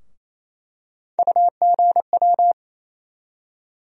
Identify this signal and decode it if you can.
Morse code, heard as a keyed tone, about 28 wpm, 700 Hz. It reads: UGW
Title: USA Call Sign Postfixes - 3 Letters 28wpm